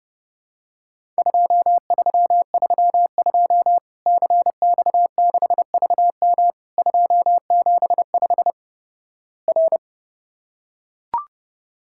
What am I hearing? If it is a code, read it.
Morse code, heard as a keyed tone, about 30 wpm, 700 Hz. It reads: 2332 CX6VM 275 R
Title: Worked All Europe DX Contest - QTC Exchange - 30wpm